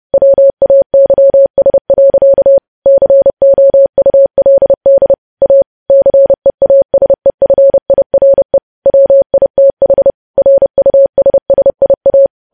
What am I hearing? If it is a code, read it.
Morse code, heard as a keyed tone, about 30 wpm, 550 Hz. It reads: WAYS. COULD A CEASEFIRE WITH RUSSIA